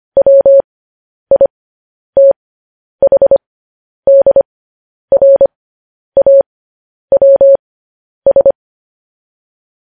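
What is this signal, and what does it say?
Morse code, heard as a keyed tone, about 25 wpm, spaced as for 10 wpm, 550 Hz. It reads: WITHDRAWS
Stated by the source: Morse transmission of news headlines.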